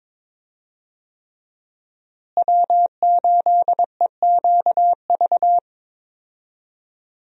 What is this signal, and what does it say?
Morse code, heard as a keyed tone, about 22 wpm, 700 Hz. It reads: W8EQV
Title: USA Call Signs - 5 Characters - Repeated 3x - 22wpm